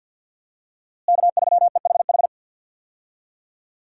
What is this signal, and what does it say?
Morse code, heard as a keyed tone, about 53 wpm, 700 Hz. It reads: K3EHH
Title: USA Call Signs - 5 Characters - Repeated 3x - 50wpm